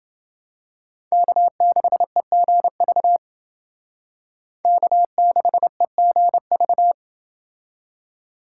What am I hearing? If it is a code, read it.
Morse code, heard as a keyed tone, about 28 wpm, 700 Hz. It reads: K6EGV K6EGV